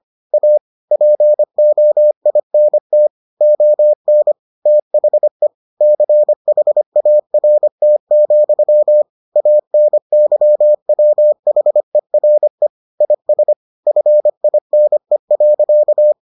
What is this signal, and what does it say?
Morse code, heard as a keyed tone, about 25 wpm, 600 Hz. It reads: A POINT ON THE CHART, ANYWHERE IS FINE.